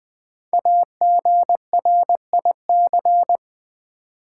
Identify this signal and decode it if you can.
Morse code, heard as a keyed tone, about 20 wpm, 700 Hz. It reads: AGRIC
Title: Build A Word - 5 Letters At A Time - 12-Letter Words - 20wpm